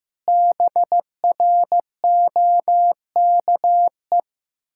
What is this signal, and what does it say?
Morse code, heard as a keyed tone, about 16 wpm, 700 Hz. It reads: BROKE
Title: Pangram Challenge v2 - 15wpm